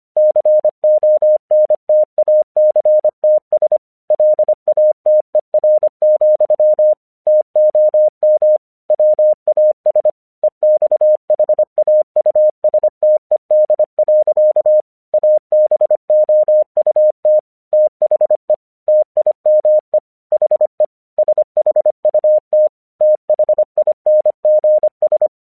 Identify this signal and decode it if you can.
Morse code, heard as a keyed tone, about 25 wpm, 610 Hz. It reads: CONTACTS LATER, TOM WAS EXHAUSTED. ABOUT THE TIME HE SHUT THINGS
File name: 1 English 25wpm / AQC34